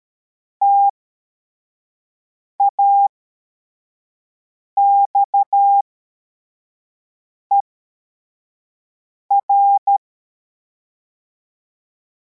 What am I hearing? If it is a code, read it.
Morse code, heard as a keyed tone, about 13 wpm, spaced as for 4 wpm, 800 Hz. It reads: TAXER
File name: formation / cw / Text44